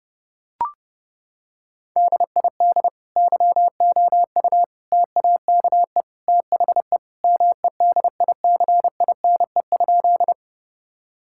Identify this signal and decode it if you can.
Morse code, heard as a keyed tone, about 30 wpm, 700 Hz. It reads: DID YOU TAKE THE MEDICINE?
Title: Sentences from Top 1000 Words - Review Part 6 - 30wpm